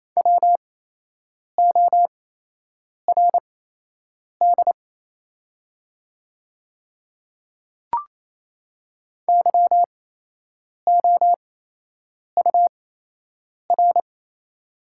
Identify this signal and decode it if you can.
Morse code, heard as a keyed tone, about 28 wpm, spaced as for 8 wpm, 700 Hz. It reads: WORD YOUR
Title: 8x Character Spacing - Top 100 Words - 28wpm